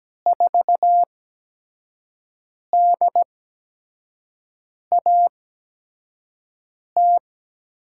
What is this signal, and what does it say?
Morse code, heard as a keyed tone, about 17 wpm, spaced as for 5 wpm, 700 Hz. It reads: 4DAT
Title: USA Call Signs - 5 Characters - 8x Character Spacing - 17wpm